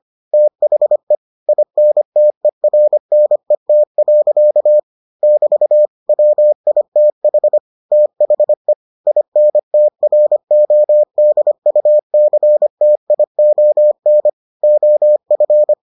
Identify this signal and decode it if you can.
Morse code, heard as a keyed tone, about 25 wpm, 600 Hz. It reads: THE INTERNET. <BT> WITH THE INTRODUCTION OF